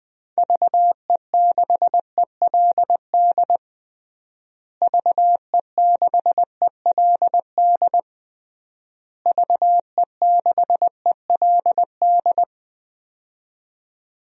Morse code, heard as a keyed tone, about 20 wpm, 700 Hz. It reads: VE6ELD VE6ELD VE6ELD
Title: DXCC-100 Call Signs - 6 Characters - Repeated 3x - 20wpm